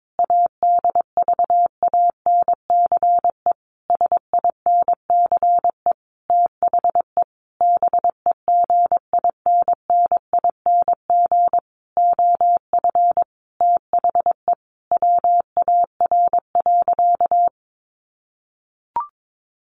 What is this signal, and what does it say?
Morse code, heard as a keyed tone, about 22 wpm, 700 Hz. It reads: ADVANCE SINCE THE BEGINNING OF THE WAR.
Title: Sentences from Top 1400 Words 22wpm